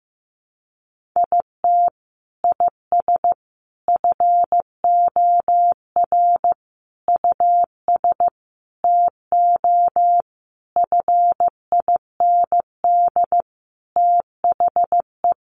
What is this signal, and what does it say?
Morse code, heard as a keyed tone, about 15 wpm, 700 Hz. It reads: IT IS FOR US TO FIND THE
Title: Sentences from Top 1300 Words 15wpm